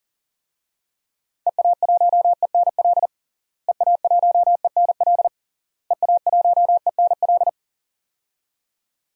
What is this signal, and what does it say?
Morse code, heard as a keyed tone, about 37 wpm, 700 Hz. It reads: EA1ENL EA1ENL EA1ENL
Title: DXCC-100 Call Signs - 6 Characters - Repeated 3x - 40wpm